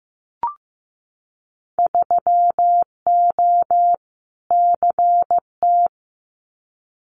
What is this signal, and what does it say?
Morse code, heard as a keyed tone, about 15 wpm, 700 Hz. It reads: 3O CT
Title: Winter Field Day Exchanges 15wpm